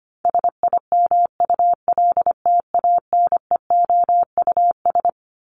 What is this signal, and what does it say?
Morse code, heard as a keyed tone, about 25 wpm, 700 Hz. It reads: SIMULTANEOUS
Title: LY Suffix - Top 100 Words with Base Words 25wpm